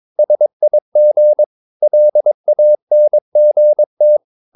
Morse code, heard as a keyed tone, about 22 wpm, 600 Hz. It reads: SIG LANGT
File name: Ferie QRS 22wpm